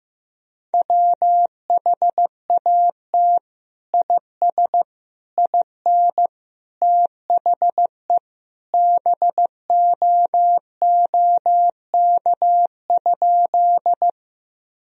Morse code, heard as a keyed tone, about 15 wpm, 700 Hz. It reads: WHAT IS IN THE BOOK?
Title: Sentences from Top 1000 Words - Review Part 1 - 15wpm